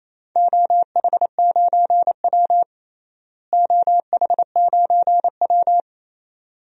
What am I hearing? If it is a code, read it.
Morse code, heard as a keyed tone, about 27 wpm, 700 Hz. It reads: OH9W OH9W
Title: DXCC-100 Call Signs - 4 Characters - Repeated 2x - 28wpm